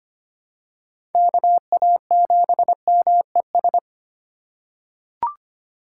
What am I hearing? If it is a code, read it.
Morse code, heard as a keyed tone, about 25 wpm, 700 Hz. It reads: KA7MES
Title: USA Call Signs - 6 Characters - 4x Character Spacing - 25wpm